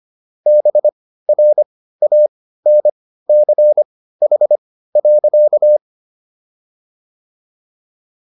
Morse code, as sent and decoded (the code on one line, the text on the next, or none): -... .-. .- -. -.-. .... .-.-.-
BRANCH.